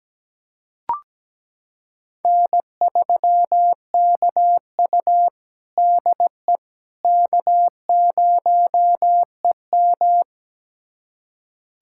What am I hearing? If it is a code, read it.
Morse code, heard as a keyed tone, about 17 wpm, 700 Hz. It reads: N3KU DE K0EM
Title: USA Callsign DE Callsign - 4 Characters - 17wpm